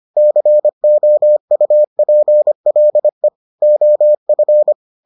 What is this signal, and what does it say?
Morse code, heard as a keyed tone, about 25 wpm, 600 Hz. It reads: COUPLE OF